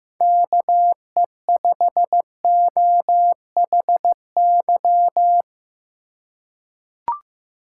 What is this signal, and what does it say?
Morse code, heard as a keyed tone, about 15 wpm, 700 Hz. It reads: KE5OHY
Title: USA Call Signs - 6 Characters - 4x Character Spacing - 15wpm